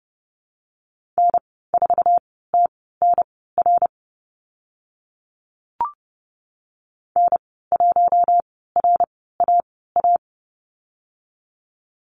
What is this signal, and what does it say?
Morse code, heard as a keyed tone, about 30 wpm, spaced as for 17 wpm, 700 Hz. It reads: N4TNR N1RAA
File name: Lesson 15g - Number 5 - Call Signs 30wpm